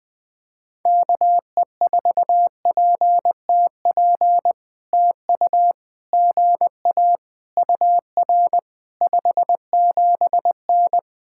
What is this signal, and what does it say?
Morse code, heard as a keyed tone, about 20 wpm, 700 Hz. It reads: KE4PTP TU GA UR 57N
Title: POTA QSO Practice - 20wpm